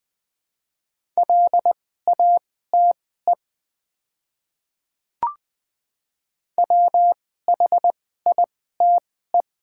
Morse code, heard as a keyed tone, about 20 wpm, spaced as for 14 wpm, 700 Hz. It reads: LATE WHITE
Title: 2x Character Spacing - Top 500 Words - 20wpm